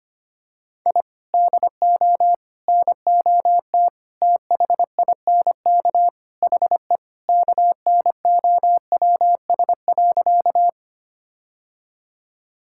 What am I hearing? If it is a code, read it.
Morse code, heard as a keyed tone, about 25 wpm, 700 Hz. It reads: I DO NOT THINK HE KNOWS.
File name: practice / 25wpm / Sentences with 4-Word Phrases - 25wpm